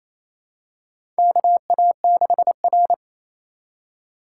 Morse code, heard as a keyed tone, about 27 wpm, 690 Hz. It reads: KA6R
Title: USA Call Signs - 4 Characters - 2x Character Spacing - 28wpm